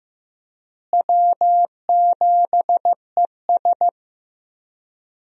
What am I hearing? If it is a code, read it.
Morse code, heard as a keyed tone, about 16 wpm, 700 Hz. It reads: W7ES
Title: USA Call Signs - 4 Characters - Repeated 2x - 15wpm